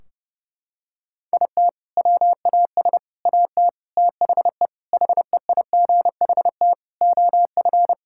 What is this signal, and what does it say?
Morse code, heard as a keyed tone, about 30 wpm, 700 Hz. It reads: IT WAS AT THE HEIGHT OF
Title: Sentences from Top 1600 Words 30wpm